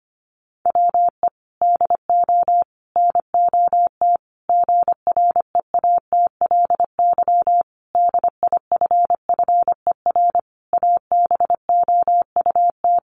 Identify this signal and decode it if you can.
Morse code, heard as a keyed tone, about 25 wpm, 700 Hz. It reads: WE DO NOT GREATLY DIFFER ABOUT
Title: Sentences from Top 1400 Words 25wpm